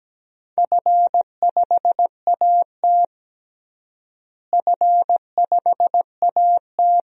Morse code, heard as a keyed tone, about 17 wpm, 700 Hz. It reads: F5AT F5AT
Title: DXCC-100 Call Signs - 4 Characters - Repeated 2x - 17wpm